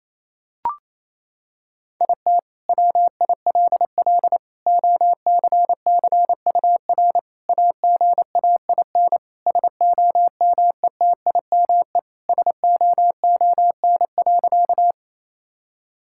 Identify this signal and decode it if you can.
Morse code, heard as a keyed tone, about 28 wpm, 700 Hz. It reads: IT WILL OCCUR AGAIN SOMETIME SOON.
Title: Sentences from Top 1000 Words - Review Part 6 - 28wpm